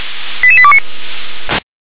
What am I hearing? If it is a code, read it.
Five-tone selective call (ZVEI), heard as quick runs of five tones, tones of about 70 ms. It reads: 88829